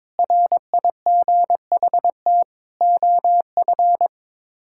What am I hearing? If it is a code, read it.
Morse code, heard as a keyed tone, about 22 wpm, 700 Hz. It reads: RIGHT OF
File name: practice / 22wpm / Sets of 8 Words - 22wpm